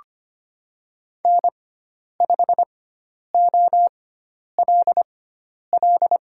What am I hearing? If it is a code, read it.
Morse code, heard as a keyed tone, about 25 wpm, spaced as for 10 wpm, 700 Hz. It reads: N5OLL
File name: Lesson 16g - Letter c - Call Signs 25wpm